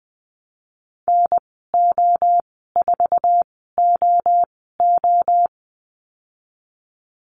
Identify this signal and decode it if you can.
Morse code, heard as a keyed tone, about 20 wpm, spaced as for 14 wpm, 700 Hz. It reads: NO4OO
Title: 200 Lesson 32g - Letter k - Call Signs 20wpm